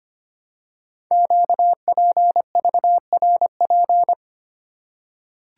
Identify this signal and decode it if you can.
Morse code, heard as a keyed tone, about 25 wpm, 700 Hz. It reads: QPVRP
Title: Code Groups - 5 Letters - 25wpm